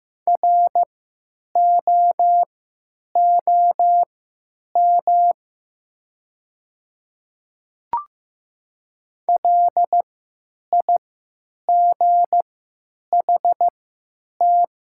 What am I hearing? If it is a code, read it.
Morse code, heard as a keyed tone, about 15 wpm, spaced as for 8 wpm, 700 Hz. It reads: ROOM LIGHT
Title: 3x Character Spacing - Top 300 Words - 15wpm